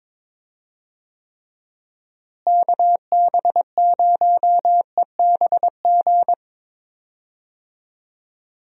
Morse code, heard as a keyed tone, about 22 wpm, 700 Hz. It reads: KB0EBG
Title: USA Call Signs - 6 Characters - Repeated 2x - 22wpm